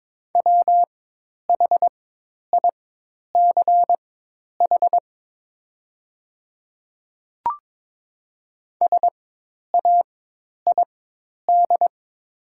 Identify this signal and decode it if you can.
Morse code, heard as a keyed tone, about 22 wpm, spaced as for 10 wpm, 700 Hz. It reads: WHICH SAID
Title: 4x Character Spacing - Top 100 Words - 22wpm